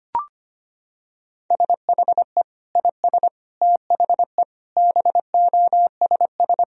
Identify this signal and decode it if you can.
Morse code, heard as a keyed tone, about 25 wpm, 700 Hz. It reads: SHE IS THE BOSS
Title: Sentences from Top 1000 Words - Review Part 4 - 25wpm